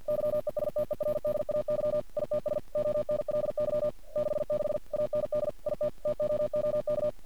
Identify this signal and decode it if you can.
Morse code, heard as a keyed tone, about 51 wpm, 600 Hz. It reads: YESTERDAY ITS ONLY 66 AND IT TOOK
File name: archived / CRSnet